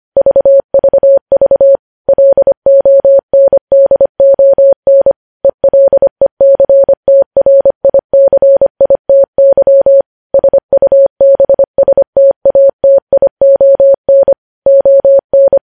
Morse code, heard as a keyed tone, about 25 wpm, 550 Hz. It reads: VVV LONDON ELECTRICITY SUBSTATION ON
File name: podcast / News headlines-1x-25x25